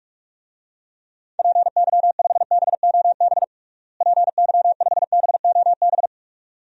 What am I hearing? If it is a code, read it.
Morse code, heard as a keyed tone, about 46 wpm, 700 Hz. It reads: PY5BOB PY5BOB
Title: DXCC-100 Call Signs - 6 Characters - Repeated 2x - 45wpm